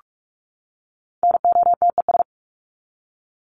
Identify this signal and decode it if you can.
Morse code, heard as a keyed tone, about 44 wpm, 700 Hz. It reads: NOTES
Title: ES:S Suffix - Top 100 Words with Base Words 45wpm